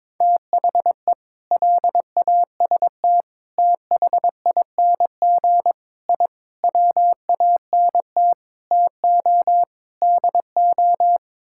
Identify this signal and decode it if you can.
Morse code, heard as a keyed tone, about 22 wpm, 700 Hz. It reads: THE LAST THING I WANT TO DO
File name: practice / 22wpm / Sets of 7 Words - 22wpm